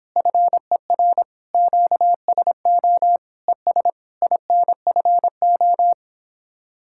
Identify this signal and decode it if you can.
Morse code, heard as a keyed tone, about 31 wpm, 700 Hz. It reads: FER QSO ES INFO